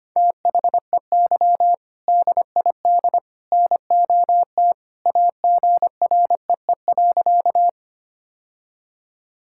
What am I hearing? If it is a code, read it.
Morse code, heard as a keyed tone, about 25 wpm, 700 Hz. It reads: THEY DID NOT AGREE.
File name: practice / 25wpm / Sentences with 3-Word Phrases - 25wpm